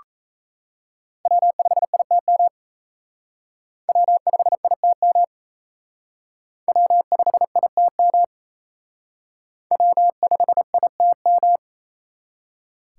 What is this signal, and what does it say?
Morse code, heard as a keyed tone, about 34 wpm, 700 Hz. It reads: W5ITM W5ITM W5ITM W5ITM
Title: Speed Racing - USA 1x3 Call Signs 28wpm